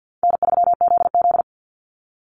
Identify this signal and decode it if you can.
Morse code, heard as a keyed tone, about 48 wpm, 700 Hz. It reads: N3ZZ
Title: Call Signs Repeated 3x - 50wpm